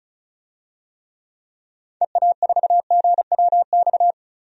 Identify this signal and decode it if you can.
Morse code, heard as a keyed tone, about 34 wpm, 700 Hz. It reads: EA4GWX
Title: DXCC-100 Call Signs - 6 Characters - Repeated 2x - 35wpm